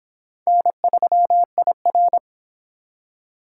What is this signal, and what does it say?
Morse code, heard as a keyed tone, about 25 wpm, 700 Hz. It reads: N3IR